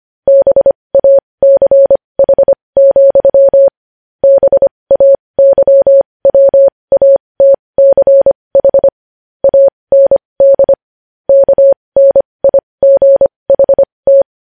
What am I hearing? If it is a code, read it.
Morse code, heard as a keyed tone, about 25 wpm, spaced as for 20 wpm, 550 Hz. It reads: BACH, BAYWATCH AND KNIGHT
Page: News Headlines in Morse Code at 20 WPM